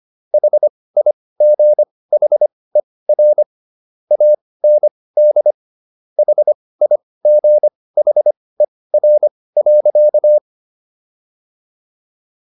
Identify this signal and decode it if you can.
Morse code, heard as a keyed tone, about 25 wpm, spaced as for 18 wpm, 600 Hz. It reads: HIGHER AND HIGHER.